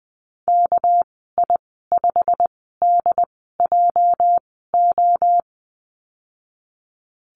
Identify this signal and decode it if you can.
Morse code, heard as a keyed tone, about 20 wpm, 700 Hz. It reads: K I 5 D J O
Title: 238 Lesson 38g - Letter q - Call Signs 20wpm